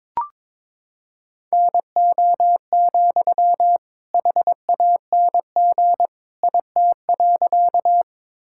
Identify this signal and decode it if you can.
Morse code, heard as a keyed tone, about 21 wpm, 700 Hz. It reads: NO, HANG IT.
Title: Sentences from Top 1600 Words 22wpm